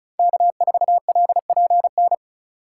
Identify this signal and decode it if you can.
Morse code, heard as a keyed tone, about 34 wpm, 700 Hz. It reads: K4LPN
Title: USA Call Signs - 5 Characters - 4x Character Spacing - 35wpm